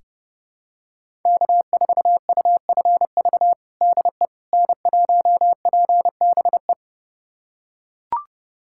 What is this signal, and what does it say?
Morse code, heard as a keyed tone, about 30 wpm, 700 Hz. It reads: K4UFV DE N1PBE
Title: USA Callsign DE Callsign - 5 Characters - 30wpm